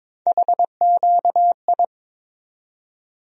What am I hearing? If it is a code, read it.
Morse code, heard as a keyed tone, about 22 wpm, 700 Hz. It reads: HQI
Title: USA Call Sign Postfixes - 3 Letters 22wpm